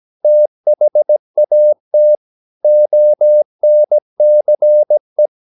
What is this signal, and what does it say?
Morse code, heard as a keyed tone, about 17 wpm, 600 Hz. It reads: THAT ONCE
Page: Romanen Never af Ken Follet på engelsk.